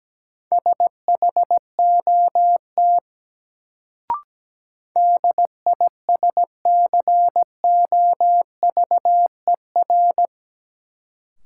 Morse code, heard as a keyed tone, about 17 wpm, 700 Hz. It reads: SHOT DISCOVER
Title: Sentences from Top 1100 Words - New Word Drill - 17wpm